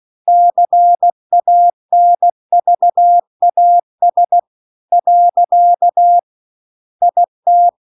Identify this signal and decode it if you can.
Morse code, heard as a keyed tone, about 16 wpm, 700 Hz. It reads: CANVAS . IT